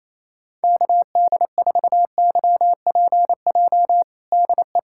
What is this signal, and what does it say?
Morse code, heard as a keyed tone, about 28 wpm, 700 Hz. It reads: KD4YPJ DE
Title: USA Callsign DE Callsign - 6 Characters - 28wpm